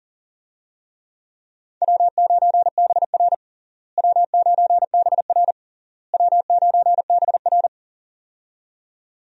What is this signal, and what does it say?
Morse code, heard as a keyed tone, about 40 wpm, 700 Hz. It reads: W9BR W9BR W9BR
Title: USA Call Signs - 4 Characters - Repeated 3x - 40wpm